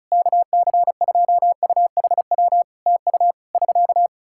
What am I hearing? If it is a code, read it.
Morse code, heard as a keyed tone, about 36 wpm, 700 Hz. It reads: KC2UHW TU <SK>
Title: Practice QSO - Exchange 4 - Ending - 35wpm